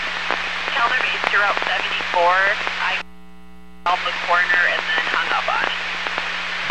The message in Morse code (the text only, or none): none